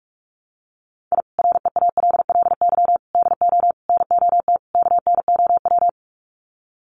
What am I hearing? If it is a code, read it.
Morse code, heard as a keyed tone, about 46 wpm, 700 Hz. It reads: I REALLY DO NOT KNOW
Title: Sets of 5 Words Encore 45wpm